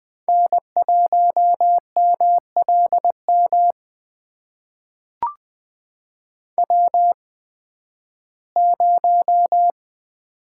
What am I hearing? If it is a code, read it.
Morse code, heard as a keyed tone, about 20 wpm, spaced as for 13 wpm, 700 Hz. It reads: N1MLM W0
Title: USA Call Signs - 5 Characters - 8x Character Spacing - 20wpm